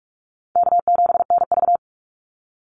Morse code, heard as a keyed tone, about 44 wpm, 700 Hz. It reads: K7NV
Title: Easy Call Signs - 45wpm